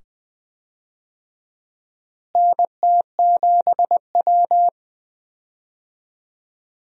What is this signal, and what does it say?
Morse code, heard as a keyed tone, about 20 wpm, 700 Hz. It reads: NT7W
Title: USA Call Signs - 4 Characters - Repeated 2x - 20wpm